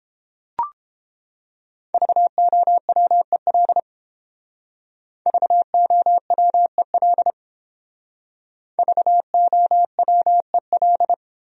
Morse code, heard as a keyed tone, about 30 wpm, 700 Hz. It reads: VOWEL VOWEL VOWEL